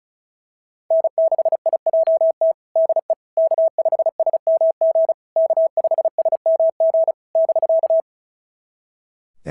Morse code, heard as a keyed tone, about 35 wpm, 650 Hz. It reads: N6IJT DE K5SMG K5SMG <BK>
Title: SOTA QSO Practice 35wpm